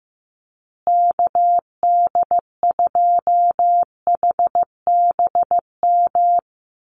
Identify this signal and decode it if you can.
Morse code, heard as a keyed tone, about 15 wpm, 700 Hz. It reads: KD2HBM
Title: USA Call Signs Repeated 3x 15wpm